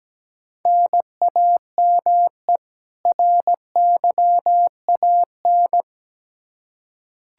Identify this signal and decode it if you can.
Morse code, heard as a keyed tone, about 17 wpm, 700 Hz. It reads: NAME RYAN
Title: K1USN SST QSO Practice 17wpm